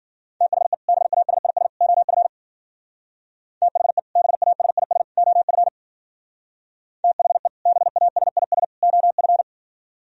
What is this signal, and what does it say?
Morse code, heard as a keyed tone, about 49 wpm, 700 Hz. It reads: THE BASIS OF THE BASIS OF THE BASIS OF
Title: Speed-Racing - Sets of 3 Words 40wpm